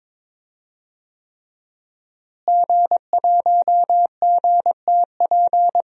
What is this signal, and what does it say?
Morse code, heard as a keyed tone, about 22 wpm, 700 Hz. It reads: G1GTP
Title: DXCC-100 Call Signs - 5 Characters - Repeated 3x - 22wpm